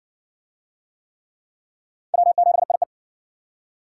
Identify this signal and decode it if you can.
Morse code, heard as a keyed tone, about 60 wpm, 700 Hz. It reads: W7IE